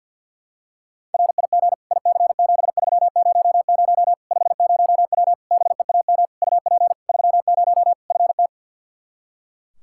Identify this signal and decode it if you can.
Morse code, heard as a keyed tone, about 47 wpm, 700 Hz. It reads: RIG IC7300 50W BEAM UP 30 FT
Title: Practice QSO - Exchange 2 - Info - 50wpm